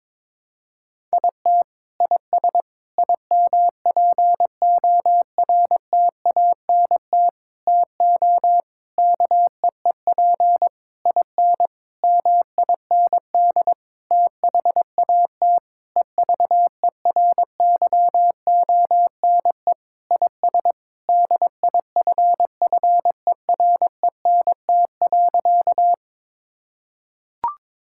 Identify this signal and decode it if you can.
Morse code, heard as a keyed tone, about 22 wpm, 700 Hz. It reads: IT IS IMPORTANT TO KEEP IN MIND THAT EVERYONE IS DIFFERENT.
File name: practice / 22wpm / Sentences with 8-Word Phrases - 22wpm